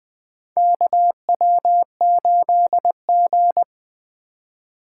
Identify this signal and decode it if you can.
Morse code, heard as a keyed tone, about 20 wpm, 700 Hz. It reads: KW8G
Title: USA Call Signs - 4 Characters - 3x Character Spacing - 20wpm